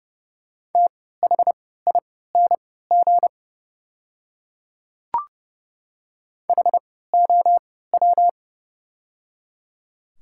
Morse code, heard as a keyed tone, about 30 wpm, spaced as for 17 wpm, 700 Hz. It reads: THING HOW
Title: 3x Character Spacing - Top 300 Words - 30wpm